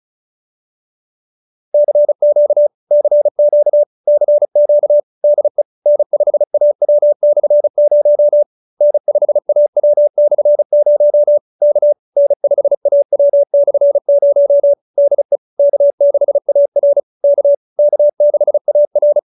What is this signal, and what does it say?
Morse code, heard as a keyed tone, about 34 wpm, 590 Hz. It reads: CQ CQ CQ DE N5AW/0 N5AW/0 K N5AW/0 DE K6AR K K6AR
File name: QSO312 35